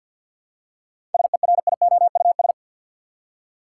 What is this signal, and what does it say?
Morse code, heard as a keyed tone, about 53 wpm, 700 Hz. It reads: SERIOUS
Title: Seven Letter Words - 50wpm